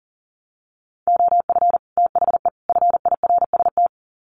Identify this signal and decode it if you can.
Morse code, heard as a keyed tone, about 40 wpm, 700 Hz. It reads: OF THE FIRST
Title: Sets of 3 Words 40wpm